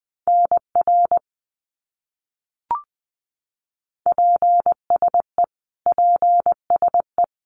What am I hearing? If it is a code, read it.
Morse code, heard as a keyed tone, about 20 wpm, 700 Hz. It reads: NR PSE PSE
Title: Abbreviations Repeated 3x - 20wpm